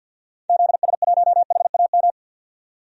Morse code, heard as a keyed tone, about 51 wpm, 700 Hz. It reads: ZS1HAM
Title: DXCC-100 Call Signs - 6 Characters - 50wpm